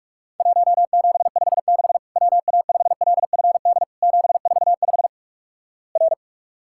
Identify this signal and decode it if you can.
Morse code, heard as a keyed tone, about 46 wpm, 700 Hz. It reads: 1756 WA5LFD 745 R
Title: Worked All Europe DX Contest - QTC Exchange - 45wpm